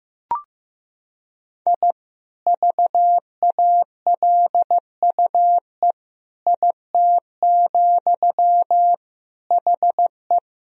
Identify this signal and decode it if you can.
Morse code, heard as a keyed tone, about 15 wpm, 700 Hz. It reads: I VALUE IT, HE
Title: Sentences from Top 1000 Words - Review Part 3 - 15wpm